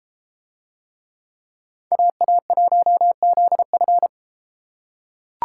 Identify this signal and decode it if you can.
Morse code, heard as a keyed tone, about 32 wpm, 700 Hz. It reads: AA1ZF